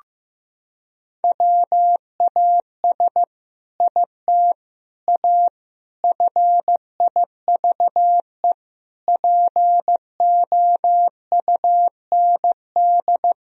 Morse code, heard as a keyed tone, about 15 wpm, 700 Hz. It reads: WAS IT A FIVE POUND
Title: Sentences from Top 1000 Words - Review Part 2 - 15wpm